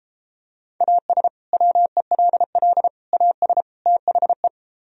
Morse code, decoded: AS WELL AS THE

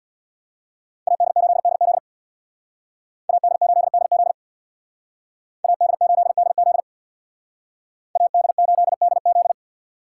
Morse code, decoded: AD8DZ AD8DZ AD8DZ AD8DZ

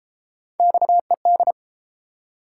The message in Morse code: XED

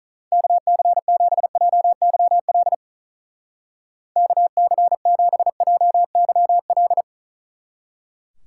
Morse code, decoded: KC7JYL KC7JYL

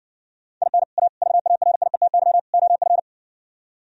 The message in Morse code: IN A VARIETY OF

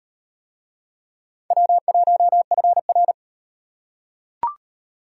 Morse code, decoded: W1FR

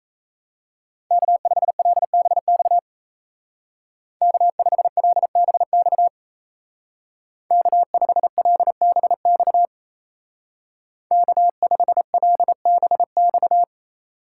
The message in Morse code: K5LBX K5LBX K5LBX K5LBX

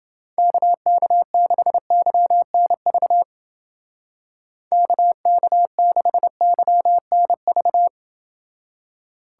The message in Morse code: KK6YNV KK6YNV

